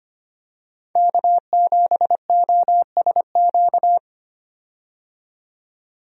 Morse code, heard as a keyed tone, about 25 wpm, 700 Hz. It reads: K7OSQ
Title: USA Call Signs - 5 Characters - 25wpm